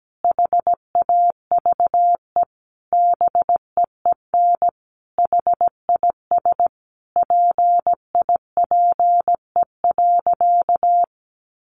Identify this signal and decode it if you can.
Morse code, heard as a keyed tone, about 17 wpm, 700 Hz. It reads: HAVE BEEN HIS PIPE.
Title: Sentences from Top 1500 Words 17wpm